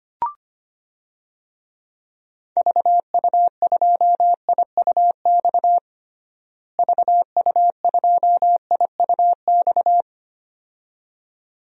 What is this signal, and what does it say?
Morse code, heard as a keyed tone, about 25 wpm, 700 Hz. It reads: VU2IUX VU2IUX
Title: DXCC-100 Call Signs - 6 Characters - Repeated 2x - 25wpm